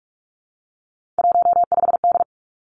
Morse code, heard as a keyed tone, about 46 wpm, 700 Hz. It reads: 15D
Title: Three Challenging Letter-Number Combinations 45wpm